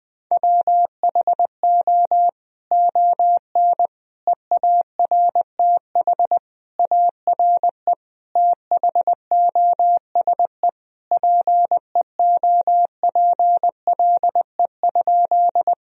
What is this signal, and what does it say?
Morse code, heard as a keyed tone, about 20 wpm, 700 Hz. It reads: WHO ON EARTH ARE THOSE PEOPLE?
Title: Sentences from Top 1000 Words - Review Part 1 - 20wpm